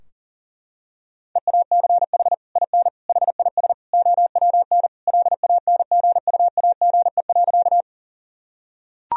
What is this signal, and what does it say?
Morse code, heard as a keyed tone, about 40 wpm, 700 Hz. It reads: EACH IN HIS OWN LANGUAGE.
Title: Sentences from Top 1000 Words - Review Part 3 - 40wpm